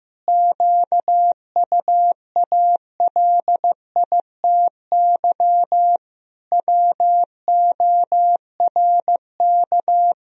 Morse code, heard as a keyed tone, about 15 wpm, 700 Hz. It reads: QUALITY WORK
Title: Sentences from Top 1000 Words - Review Part 4 - 15wpm